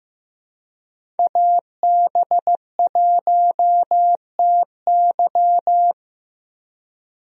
Morse code, heard as a keyed tone, about 15 wpm, 700 Hz. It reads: AB1TY
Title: USA Call Signs - 5 Characters - Repeated 2x - 15wpm